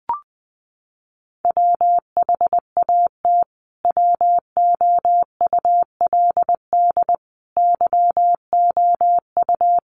Morse code, heard as a keyed tone, about 20 wpm, 700 Hz. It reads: WHAT WOULD YOU
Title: Sentences from Top 300 Words 20wpm